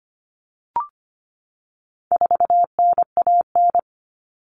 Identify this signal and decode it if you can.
Morse code, heard as a keyed tone, about 25 wpm, 700 Hz. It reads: 4NAN
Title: Common Cut Numbers - 4-Digit 25wpm